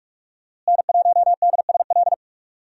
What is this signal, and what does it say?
Morse code, heard as a keyed tone, about 46 wpm, 700 Hz. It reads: N1DSL
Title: USA Call Signs - 5 Characters - Repeated 2x - 45wpm